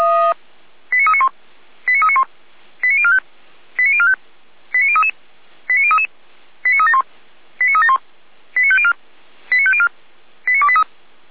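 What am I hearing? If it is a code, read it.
Five-tone selective call (ZVEI), heard as quick runs of five tones, tones of about 80 ms. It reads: 89391 89391 89045 89045 89033 89033 89371 89371 89504 89504 89283
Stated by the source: ZVEI five-tone selective call received on SDR.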